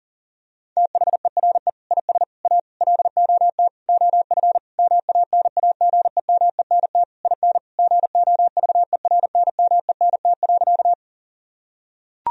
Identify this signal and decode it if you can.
Morse code, heard as a keyed tone, about 37 wpm, 700 Hz. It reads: THERE IS A LOT OF MANAGEMENT IN GOVERNMENT.
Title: Sentences from Top 1000 Words - Review Part 4 - 40wpm